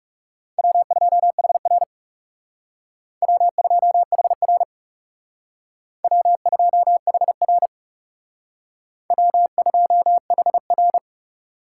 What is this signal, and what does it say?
Morse code, heard as a keyed tone, about 37 wpm, 700 Hz. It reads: W2HR W2HR W2HR W2HR
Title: Speed Racing - USA 1x2 Call Signs 30wpm